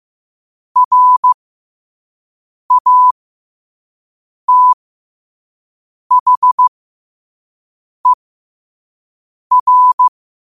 Morse code, heard as a keyed tone, about 15 wpm, spaced as for 5 wpm, 1000 Hz. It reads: RATHER
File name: qotdfiles / 5wpm